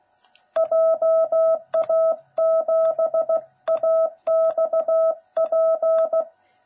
Morse code, heard as a keyed tone, about 16 wpm, 660 Hz. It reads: JA7AXP
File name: jh8cbh / 2021cw / cw09b